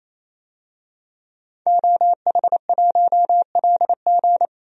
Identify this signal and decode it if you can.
Morse code, heard as a keyed tone, about 27 wpm, 700 Hz. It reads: OH1LG
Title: DXCC-100 Call Signs - 5 Characters - 28wpm